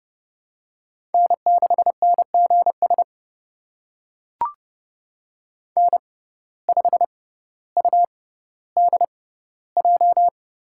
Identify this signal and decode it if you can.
Morse code, heard as a keyed tone, about 30 wpm, spaced as for 15 wpm, 700 Hz. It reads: N6NGS N5UDJ
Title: USA Call Signs - 6x Character Spacing - 30wpm